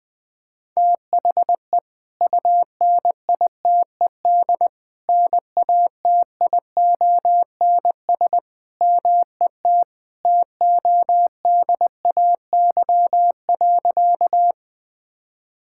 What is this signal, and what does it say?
Morse code, heard as a keyed tone, about 20 wpm, 700 Hz. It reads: THE UNITED NATIONS MET TODAY.
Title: Sentences with 3-Word Phrases - 20wpm